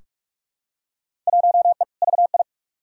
Morse code, heard as a keyed tone, about 42 wpm, 700 Hz. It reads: 1E VI
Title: Field Day Exchanges - 45wpm